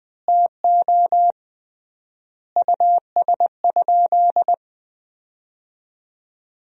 Morse code, heard as a keyed tone, about 20 wpm, 700 Hz. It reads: TO US?
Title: 3x Word-Spacing - Sentences from Top 400 Words - 20wpm